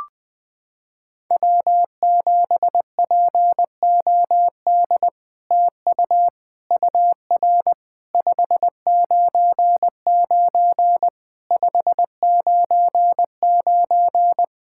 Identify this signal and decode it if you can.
Morse code, heard as a keyed tone, about 20 wpm, 700 Hz. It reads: W7POD TU UR 599 599
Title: SOTA QSO Practice 20wpm